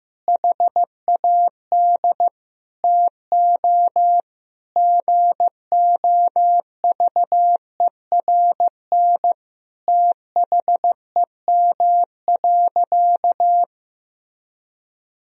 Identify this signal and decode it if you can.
Morse code, heard as a keyed tone, about 15 wpm, 700 Hz. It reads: HAD TO GOVERN THEM.
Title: Sentences from Top 1000 Words - Review Part 2 - 15wpm